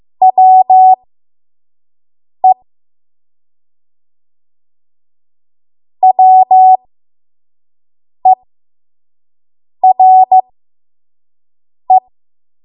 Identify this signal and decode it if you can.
Morse code, heard as a keyed tone, about 15 wpm, spaced as for 5 wpm, 750 Hz. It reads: WE WERE
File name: Archive / 5 WPM / 05WPM